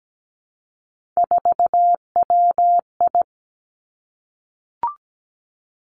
Morse code, heard as a keyed tone, about 17 wpm, 700 Hz. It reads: 4WI
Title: Three Challenging Letter-Number Combinations 17wpm